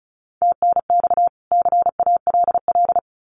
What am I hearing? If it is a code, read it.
Morse code, heard as a keyed tone, about 34 wpm, 700 Hz. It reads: TNX CALL
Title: Common QSO Phrases 35wpm